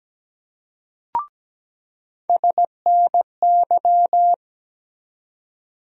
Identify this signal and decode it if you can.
Morse code, heard as a keyed tone, about 17 wpm, 700 Hz. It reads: SNY
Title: USA Call Sign Postfixes - 3 Letters 17wpm